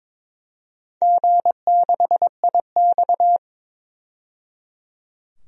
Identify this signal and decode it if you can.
Morse code, heard as a keyed tone, about 22 wpm, 700 Hz. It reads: G6IX
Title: DXCC-100 Call Signs - 4 Characters - 22wpm